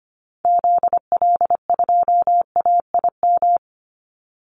Morse code, heard as a keyed tone, about 25 wpm, 700 Hz. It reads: ZL2AIM
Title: Call Signs Repeated 2x - 25wpm